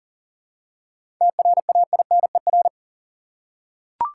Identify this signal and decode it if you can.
Morse code, heard as a keyed tone, about 37 wpm, 700 Hz. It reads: TRAINER
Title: Seven Letter Words - 40wpm